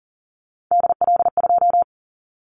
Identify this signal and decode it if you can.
Morse code, heard as a keyed tone, about 40 wpm, 700 Hz. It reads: DL2
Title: Callsign Prefixes for Top 25 DXCC - 40wpm